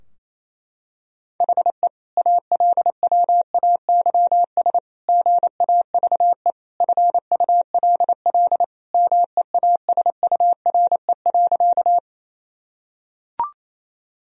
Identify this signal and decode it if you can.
Morse code, heard as a keyed tone, about 28 wpm, 700 Hz. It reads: HE ALWAYS GAVE FULL MEASURE.
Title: Sentences from Top 1000 Words - Review Part 3 - 28wpm